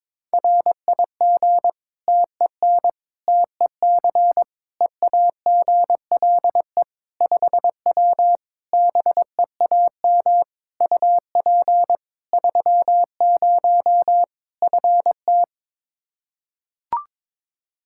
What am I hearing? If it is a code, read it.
Morse code, heard as a keyed tone, about 22 wpm, 700 Hz. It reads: RIG TEN TEC EAGLE 5W BEAM UP 30 FT
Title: Practice QSO - Exchange 2 - Info - 22wpm